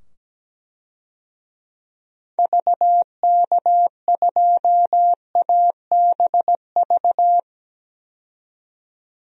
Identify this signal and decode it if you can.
Morse code, heard as a keyed tone, about 17 wpm, 700 Hz. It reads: VK2ABV
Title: DXCC-100 Call Signs - 6 Characters - 17wpm